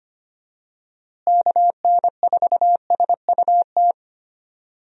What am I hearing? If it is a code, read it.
Morse code, heard as a keyed tone, about 25 wpm, 700 Hz. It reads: KN4SUT
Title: USA Call Signs - 6 Characters - Repeated 2x - 25wpm